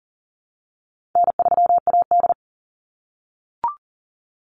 Morse code, heard as a keyed tone, about 40 wpm, 700 Hz. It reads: N3AD
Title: Call Signs - 40wpm